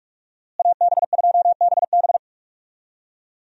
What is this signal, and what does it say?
Morse code, heard as a keyed tone, about 46 wpm, 700 Hz. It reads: AB2BB
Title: USA Call Signs - 5 Characters - Repeated 3x - 45wpm